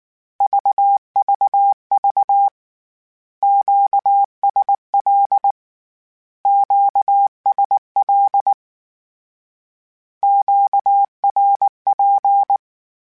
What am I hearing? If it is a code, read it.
Morse code, heard as a keyed tone, about 19 wpm, 800 Hz. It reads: VVV QSL QSL QRP
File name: Q Calls / Q Calls-mp3s / Q Calls Twice @18